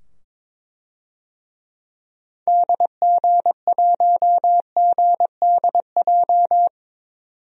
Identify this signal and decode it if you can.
Morse code, heard as a keyed tone, about 23 wpm, 700 Hz. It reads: DG1GDJ
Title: DXCC-100 Call Signs - 6 Characters - 22wpm